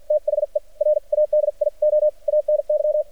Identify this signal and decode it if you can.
Morse code, heard as a keyed tone, about 47 wpm, spaced as for 41 wpm, 600 Hz. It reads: THE RADIO ANY
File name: archived / CRSnet